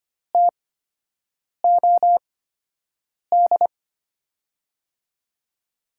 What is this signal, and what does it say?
Morse code, heard as a keyed tone, about 25 wpm, spaced as for 7 wpm, 700 Hz. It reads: TOD
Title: USA Call Signs - 5 Characters - 8x Character Spacing - 25wpm